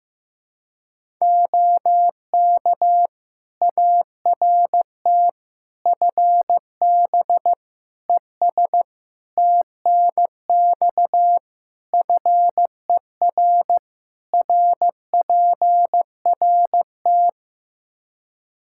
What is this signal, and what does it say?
Morse code, heard as a keyed tone, about 15 wpm, 700 Hz. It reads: OK ART FB ES TNX FER RPRT
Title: Practice QSO - Exchange 2 - Info - 15wpm